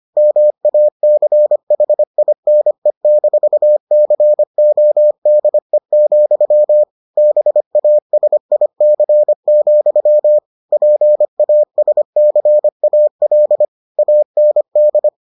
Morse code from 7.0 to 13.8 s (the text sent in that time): BASIC, PASCAL